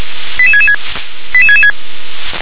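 Five-tone selective call (ZVEI), heard as quick runs of five tones, tones of about 70 ms. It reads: 88666 88666
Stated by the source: ZVEI five-tone selective call received on an SDR.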